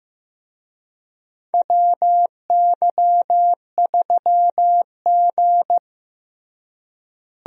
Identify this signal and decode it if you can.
Morse code, heard as a keyed tone, about 15 wpm, 700 Hz. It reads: WY3G
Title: USA Call Signs - 4 Characters - Repeated 3x - 15wpm